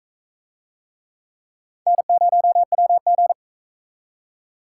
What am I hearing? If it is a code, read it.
Morse code, heard as a keyed tone, about 42 wpm, 700 Hz. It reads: N0WG